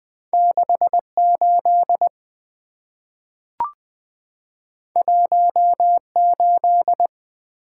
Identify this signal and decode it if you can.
Morse code, heard as a keyed tone, about 20 wpm, 700 Hz. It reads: 68 18
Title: Two Letter-Number Combinations - 20wpm